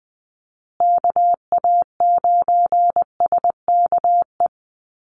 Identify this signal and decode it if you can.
Morse code, heard as a keyed tone, about 20 wpm, 700 Hz. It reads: KA9SKE
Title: USA Call Signs Repeated 2x 20wpm